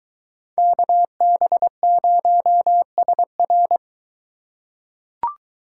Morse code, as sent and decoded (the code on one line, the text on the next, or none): -.- -... ----- ... .-.
KB0SR